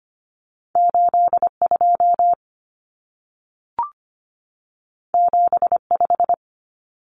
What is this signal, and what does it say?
Morse code, heard as a keyed tone, about 25 wpm, 700 Hz. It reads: 82 75
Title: Numbers - 2-Digit 25wpm